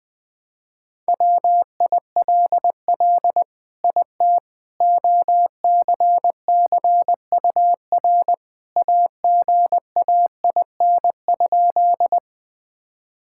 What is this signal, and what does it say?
Morse code, heard as a keyed tone, about 20 wpm, 700 Hz. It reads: WILL IT OCCUR AGAIN?
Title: Sentences from Top 1000 Words - Review Part 6 - 20wpm